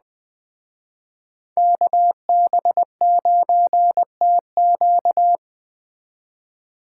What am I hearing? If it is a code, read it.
Morse code, heard as a keyed tone, about 20 wpm, 700 Hz. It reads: KB9TQ